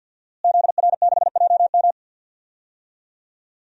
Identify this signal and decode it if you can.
Morse code, heard as a keyed tone, about 52 wpm, 700 Hz. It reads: ZR6JM
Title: DXCC-100 Call Signs - 5 Characters - Repeated 2x - 50wpm